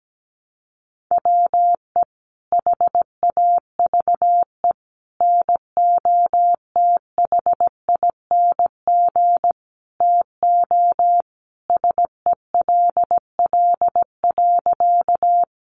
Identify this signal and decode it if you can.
Morse code, heard as a keyed tone, about 17 wpm, 700 Hz. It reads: WE HAVE NOTHING TO SELL.
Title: Sentences from Top 1100 Words 17wpm